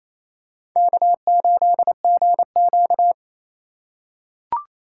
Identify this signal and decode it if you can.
Morse code, heard as a keyed tone, about 27 wpm, 700 Hz. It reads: K8GQ
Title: USA Call Signs - 4 Characters - 3x Character Spacing - 28wpm